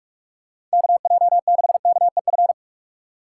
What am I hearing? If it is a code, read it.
Morse code, heard as a keyed tone, about 43 wpm, 700 Hz. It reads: KJ6KEF
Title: USA Call Signs - 6 Characters - 4x Character Spacing - 45wpm